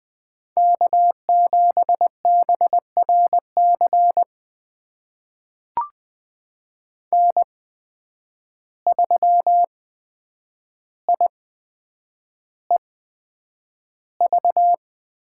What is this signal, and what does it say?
Morse code, heard as a keyed tone, about 20 wpm, spaced as for 9 wpm, 700 Hz. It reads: K7BRC N3IEV
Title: USA Call Signs - 8x Character Spacing - 20wpm